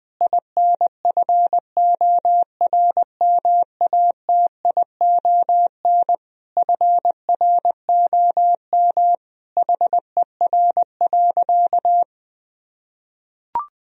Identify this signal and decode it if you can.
Morse code, heard as a keyed tone, about 20 wpm, 700 Hz. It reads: INFORMATION FROM HER.
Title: Sentences from Top 1000 Words - Review Part 3 - 20wpm